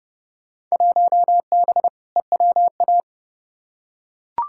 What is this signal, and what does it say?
Morse code, heard as a keyed tone, about 30 wpm, 700 Hz. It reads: 1B EWA
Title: Field Day Exchanges Encore - 30wpm